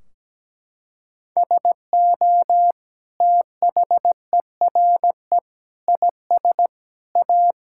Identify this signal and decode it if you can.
Morse code, heard as a keyed tone, about 17 wpm, 700 Hz. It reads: SO THERE IS A
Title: Sentences from Top 1600 Words 17wpm